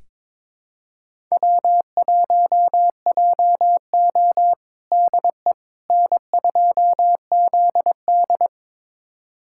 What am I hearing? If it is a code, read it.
Morse code, heard as a keyed tone, about 22 wpm, 700 Hz. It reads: W1JO DE N2ZD
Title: USA Callsign DE Callsign - 4 Characters - 22wpm